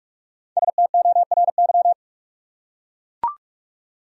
Morse code, heard as a keyed tone, about 44 wpm, 700 Hz. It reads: STORY